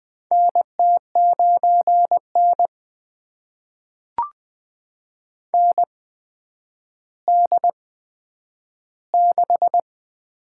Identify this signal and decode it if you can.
Morse code, heard as a keyed tone, about 20 wpm, spaced as for 10 wpm, 700 Hz. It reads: NT9N ND6
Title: USA Call Signs - 4 Characters - 8x Character Spacing - 20wpm